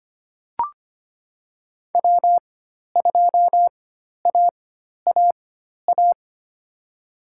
Morse code, heard as a keyed tone, about 25 wpm, spaced as for 12 wpm, 700 Hz. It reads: W2AAA
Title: USA Call Signs - 4x Character Spacing - 25wpm